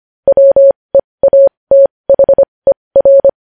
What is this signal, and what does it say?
Morse code, heard as a keyed tone, about 25 wpm, spaced as for 20 wpm, 550 Hz. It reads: WEATHER
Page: news-headlines-in-morse-code-at-20-wpm